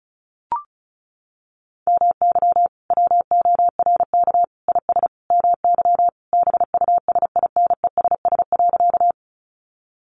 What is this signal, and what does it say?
Morse code, heard as a keyed tone, about 34 wpm, 700 Hz. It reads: MY WORK IS MY BUSINESS.
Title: Sentences from Top 600 Words 35wpm